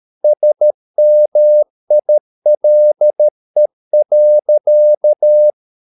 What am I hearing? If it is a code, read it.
Morse code, heard as a keyed tone, about 13 wpm, 600 Hz. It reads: SMILE.